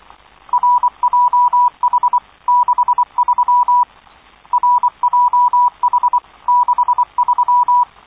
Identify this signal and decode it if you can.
Morse code, heard as a keyed tone, about 24 wpm, 980 Hz. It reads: RJH63 RJH63